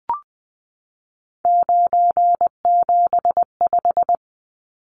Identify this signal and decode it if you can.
Morse code, heard as a keyed tone, about 20 wpm, 700 Hz. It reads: 975
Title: Numbers - 3-Digit 20wpm